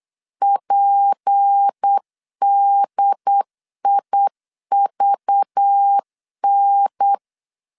Morse code, heard as a keyed tone, about 8 wpm, 800 Hz. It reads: PDIVN
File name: LEZIONI / 6 PKX0 / 6 P 8wpm 3025